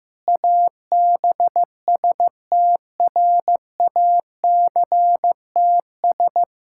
Morse code, heard as a keyed tone, about 15 wpm, 700 Hz. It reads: ABSTRACTS
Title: Pangram Challenge v2 - 15wpm